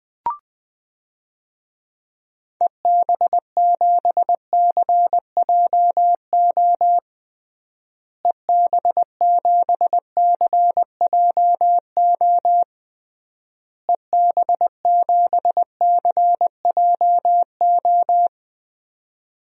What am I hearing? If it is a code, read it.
Morse code, heard as a keyed tone, about 20 wpm, 700 Hz. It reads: EB7CJO EB7CJO EB7CJO
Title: DXCC-100 Call Signs - 6 Characters - Repeated 3x - 20wpm